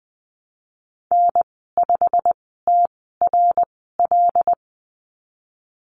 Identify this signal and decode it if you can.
Morse code, heard as a keyed tone, about 20 wpm, spaced as for 14 wpm, 700 Hz. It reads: N5TRL
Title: 088 Lesson 15g - Number 5 - Call Signs 20wpm